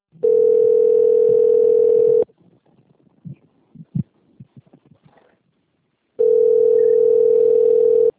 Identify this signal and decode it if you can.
Ringback tone (North America), 440 + 480 Hz.
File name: out- - - - - .41274